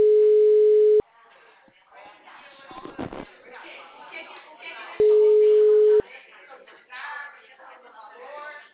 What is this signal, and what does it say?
Ringback tone (Europe, 425 Hz), with speech.